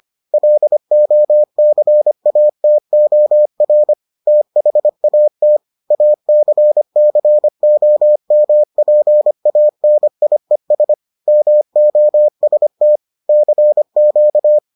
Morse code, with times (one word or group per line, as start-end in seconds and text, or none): 0.3-3.9 LOCATOR
4.3-5.6 THAT
5.9-10.9 ACCOMPANIES
11.3-13.0 MOST
13.3-14.6 CQ